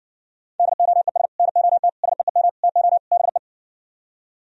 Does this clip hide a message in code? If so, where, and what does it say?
Yes: Morse code from 0.6 to 3.4 s, heard as a keyed tone, about 59 wpm, 700 Hz. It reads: DOES NOT SEEM TO BE